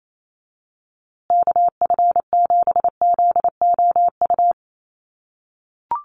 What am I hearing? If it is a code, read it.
Morse code, heard as a keyed tone, about 28 wpm, 700 Hz. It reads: KF7ZOU
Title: USA Call Signs Repeated 3x 28wpm